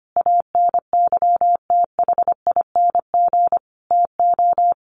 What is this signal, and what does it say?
Morse code, heard as a keyed tone, about 25 wpm, 700 Hz. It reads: ANYTHING TO
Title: Sentences from Top 1100 Words 25wpm